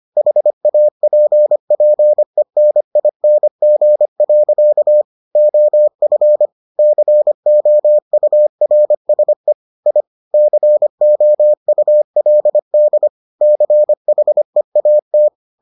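Morse code, heard as a keyed tone, about 25 wpm, 600 Hz. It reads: HAPPENING. OF COURSE I COULD CHEAT
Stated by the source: Morse edition of the Foundations of Amateur Radio episodes.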